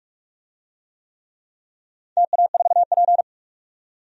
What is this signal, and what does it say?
Morse code, heard as a keyed tone, about 42 wpm, 700 Hz. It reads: TA4P
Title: DXCC-100 Call Signs - 4 Characters - 45wpm